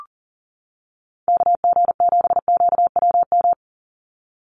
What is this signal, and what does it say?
Morse code, heard as a keyed tone, about 40 wpm, 700 Hz. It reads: KG7QWM
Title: USA Call Signs 40wpm